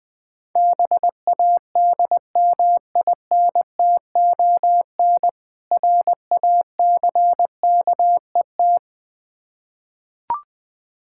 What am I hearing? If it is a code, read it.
Morse code, heard as a keyed tone, about 21 wpm, 700 Hz. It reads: BADMINTON RACKET
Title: Difficult Words with Hints - 20wpm